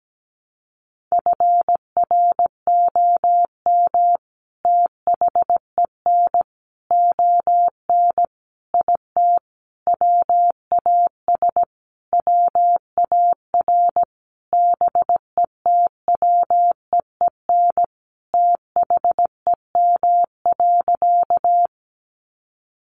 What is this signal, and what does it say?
Morse code, heard as a keyed tone, about 17 wpm, 700 Hz. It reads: FROM THEN ON IT WAS WAR BETWEEN THEM.
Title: Sentences from Top 400 Words 17wpm